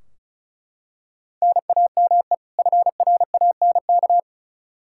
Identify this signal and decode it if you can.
Morse code, heard as a keyed tone, about 36 wpm, 700 Hz. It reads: NAME FRANK
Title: Practice QSO - Exchange 1 - Beginning - 35wpm